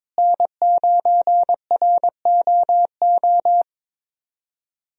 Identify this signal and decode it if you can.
Morse code, heard as a keyed tone, about 22 wpm, 700 Hz. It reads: N9ROO
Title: USA Call Signs - 5 Characters - Repeated 3x - 22wpm